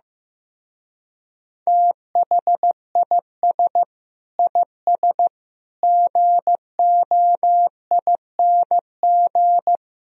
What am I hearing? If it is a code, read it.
Morse code, heard as a keyed tone, about 16 wpm, 700 Hz. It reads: THIS IS GOING
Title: Speed-Racing - Sets of 4 Words 15wpm